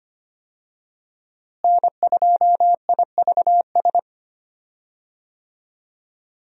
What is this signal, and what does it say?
Morse code, heard as a keyed tone, about 25 wpm, 700 Hz. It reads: N2IVS
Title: USA Call Signs - 5 Characters - Repeated 2x - 25wpm